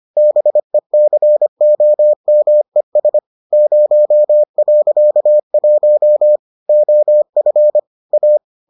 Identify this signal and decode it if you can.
Morse code, heard as a keyed tone, about 25 wpm, 600 Hz. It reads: BECOMES 0.1 OF A